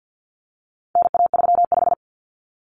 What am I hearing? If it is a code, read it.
Morse code, heard as a keyed tone, about 50 wpm, 700 Hz. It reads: NA35
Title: Common Cut Numbers - 4-Digit 50wpm